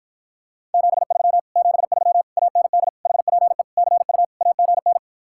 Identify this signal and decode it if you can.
Morse code, heard as a keyed tone, about 52 wpm, 700 Hz. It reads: 73 73 AND HPE CU AGN